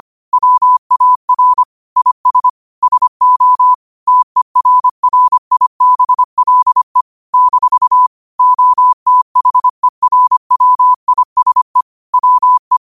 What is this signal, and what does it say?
Morse code, heard as a keyed tone, about 25 wpm, 1000 Hz. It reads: WAR IS SO TERRIBLE - OTHERWISE WE
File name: qotdfiles / 25wpm